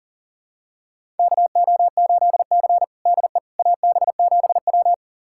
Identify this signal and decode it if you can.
Morse code, heard as a keyed tone, about 37 wpm, 700 Hz. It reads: KO8C DE AB7W
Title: USA Callsign DE Callsign - 4 Characters - 40wpm